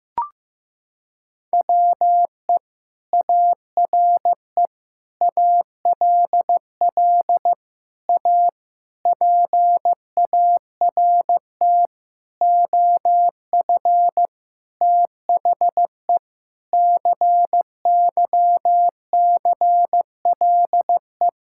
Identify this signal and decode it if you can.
Morse code, heard as a keyed tone, about 15 wpm, 700 Hz. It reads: WE ARE ALL A PART OF THE CYCLE
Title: Sentences from Top 1000 Words - Review Part 6 - 15wpm